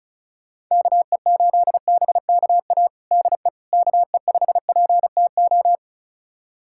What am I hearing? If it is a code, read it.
Morse code, heard as a keyed tone, about 36 wpm, 700 Hz. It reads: KE8BKA DE KE5PTO
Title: USA Callsign DE Callsign - 6 Characters - 35wpm